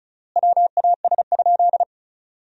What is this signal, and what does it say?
Morse code, heard as a keyed tone, about 36 wpm, 700 Hz. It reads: WAS?
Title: 3x Word-Spacing - Sentences from Top 200 Words - 35wpm